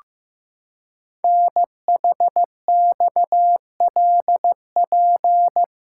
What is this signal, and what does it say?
Morse code, heard as a keyed tone, about 15 wpm, 700 Hz. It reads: NHXLP
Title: Code Groups - 5 Letter:Numbers - 15wpm